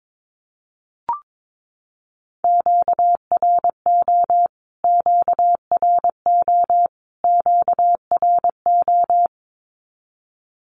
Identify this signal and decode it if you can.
Morse code, heard as a keyed tone, about 22 wpm, 700 Hz. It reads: QRO QRO QRO
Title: Q-Codes Repeated 3x 22wpm